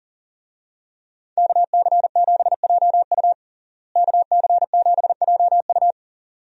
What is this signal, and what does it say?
Morse code, heard as a keyed tone, about 40 wpm, 700 Hz. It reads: KC7JU KC7JU
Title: USA Call Signs - 5 Characters - Repeated 2x - 40wpm